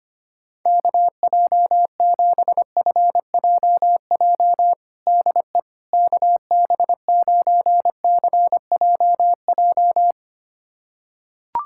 Morse code, heard as a keyed tone, about 25 wpm, 700 Hz. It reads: KJ7FJJ DE KB9CJJ
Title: USA Callsign DE Callsign - 6 Characters - 25wpm